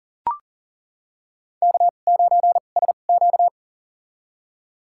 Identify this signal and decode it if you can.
Morse code, heard as a keyed tone, about 38 wpm, spaced as for 28 wpm, 700 Hz. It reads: K9SQ
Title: USA Call Signs - 4 Characters - 2x Character Spacing - 40wpm